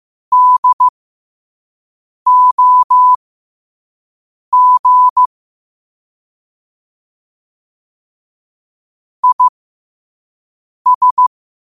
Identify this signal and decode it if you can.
Morse code, heard as a keyed tone, about 15 wpm, spaced as for 5 wpm, 1000 Hz. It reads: DOG IS